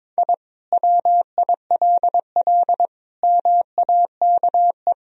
Morse code, heard as a keyed tone, about 22 wpm, 700 Hz. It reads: I WILL MAKE
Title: Sentences from Top 1000 Words - Review Part 4 - 22wpm